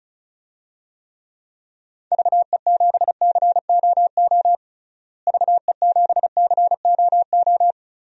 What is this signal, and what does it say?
Morse code, heard as a keyed tone, about 35 wpm, 700 Hz. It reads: VE7COO VE7COO
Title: DXCC-100 Call Signs - 6 Characters - Repeated 3x - 35wpm